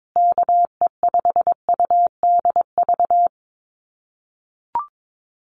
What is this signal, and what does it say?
Morse code, heard as a keyed tone, about 22 wpm, 700 Hz. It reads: KE5UDV
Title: USA Call Signs Repeated 2x 22wpm